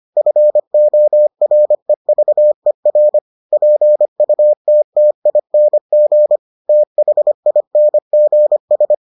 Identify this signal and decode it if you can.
Morse code, heard as a keyed tone, about 25 wpm, 600 Hz. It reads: FOREVER PUTTING THINGS